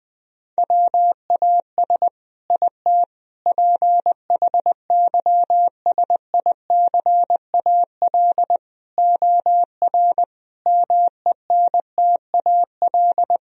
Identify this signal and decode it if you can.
Morse code, heard as a keyed tone, about 20 wpm, 700 Hz. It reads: WAS IT PHYSICAL OR MENTAL
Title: Sentences from Top 1000 Words - Review Part 4 - 20wpm